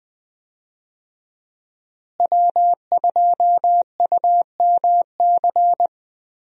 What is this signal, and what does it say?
Morse code, heard as a keyed tone, about 20 wpm, 700 Hz. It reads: W2UMC
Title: USA Call Signs - 5 Characters - Repeated 3x - 20wpm